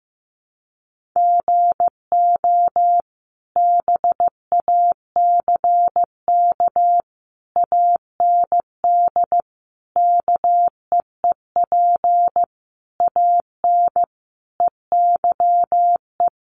Morse code, heard as a keyed tone, about 15 wpm, 700 Hz. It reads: GO BACK AND KEEP AN EYE
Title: Sentences from Top 300 Words 15wpm